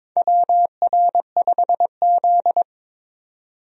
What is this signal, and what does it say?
Morse code, heard as a keyed tone, about 22 wpm, 700 Hz. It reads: WR5Z
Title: USA Call Signs - 4 Characters - 22wpm